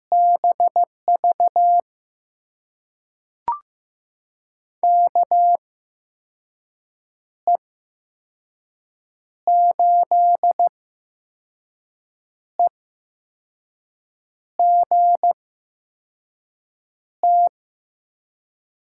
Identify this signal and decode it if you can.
Morse code, heard as a keyed tone, about 15 wpm, spaced as for 5 wpm, 700 Hz. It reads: BV KE8EGT
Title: USA Call Signs - 8x Character Spacing - 15wpm